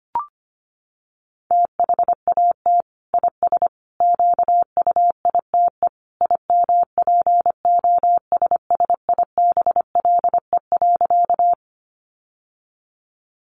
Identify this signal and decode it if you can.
Morse code, heard as a keyed tone, about 25 wpm, 700 Hz. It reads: THAT IS QUITE IMPOSSIBLE.
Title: Sentences from Top 1100 Words 25wpm